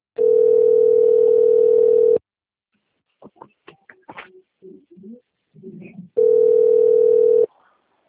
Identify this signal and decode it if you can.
Ringback tone (North America), 440 + 480 Hz.